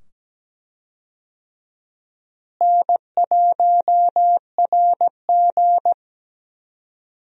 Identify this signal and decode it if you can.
Morse code, heard as a keyed tone, about 17 wpm, 700 Hz. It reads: N1RG
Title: USA Call Signs - 4 Characters - Repeated 2x - 17wpm